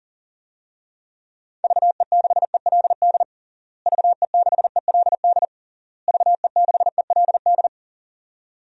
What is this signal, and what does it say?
Morse code, heard as a keyed tone, about 40 wpm, 700 Hz. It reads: VE6ELD VE6ELD VE6ELD
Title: DXCC-100 Call Signs - 6 Characters - Repeated 3x - 40wpm